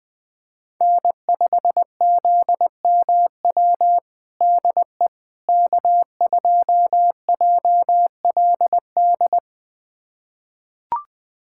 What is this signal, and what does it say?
Morse code, heard as a keyed tone, about 21 wpm, 700 Hz. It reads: N5ZMW DE K2JLD
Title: USA Callsign DE Callsign - 5 Characters - 20wpm